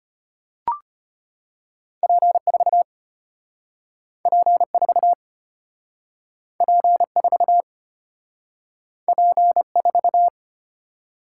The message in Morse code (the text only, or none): P4 P4 P4 P4